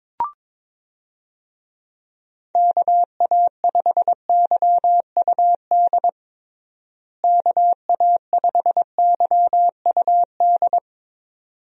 Morse code, heard as a keyed tone, about 22 wpm, 700 Hz. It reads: KA5YUD KA5YUD
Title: USA Call Signs - 6 Characters - Repeated 2x - 22wpm